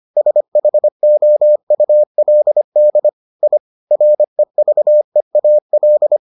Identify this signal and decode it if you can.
Morse code, heard as a keyed tone, about 25 wpm, 600 Hz. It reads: SHOULD I REVEAL